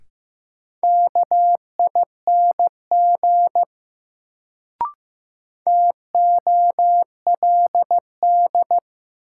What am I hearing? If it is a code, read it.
Morse code, heard as a keyed tone, about 15 wpm, 700 Hz. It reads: KING TOLD
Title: Sentences from Top 400 Words - New Word Drill - 15wpm